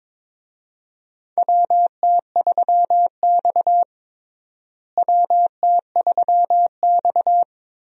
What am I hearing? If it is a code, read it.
Morse code, heard as a keyed tone, about 22 wpm, 700 Hz. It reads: WT3X WT3X
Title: USA Call Signs - 4 Characters - Repeated 2x - 22wpm